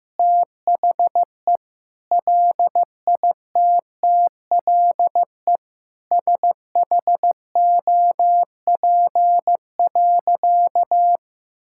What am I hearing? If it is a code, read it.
Morse code, heard as a keyed tone, about 15 wpm, 700 Hz. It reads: THE LITTLE SHOP.
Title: Sentences from Top 1600 Words 15wpm